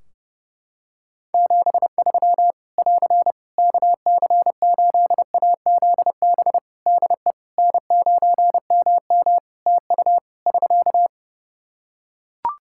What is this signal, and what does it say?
Morse code, heard as a keyed tone, about 32 wpm, 700 Hz. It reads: 73 <AR> KC8AZB DE N9MM TU <SK>
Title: K1USN SST QSO Practice 30wpm